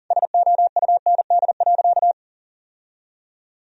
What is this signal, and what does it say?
Morse code, heard as a keyed tone, about 40 wpm, 700 Hz. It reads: SOUND.
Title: Sentences from Top 1000 Words - Review Part 1 - 40wpm